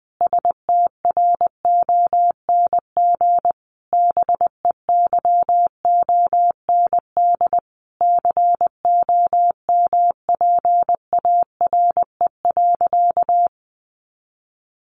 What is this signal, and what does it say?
Morse code, heard as a keyed tone, about 19 wpm, 700 Hz. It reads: STRONG BEYOND COMPARE.
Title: Sentences from Top 1300 Words 20wpm